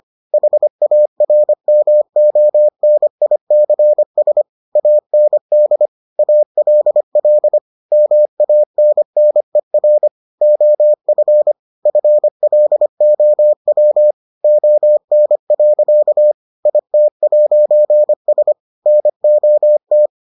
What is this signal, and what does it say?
Morse code, heard as a keyed tone, about 25 wpm, 600 Hz. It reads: HARMONICS AND ALL MANNER OF FLOW ON. IT'S NOT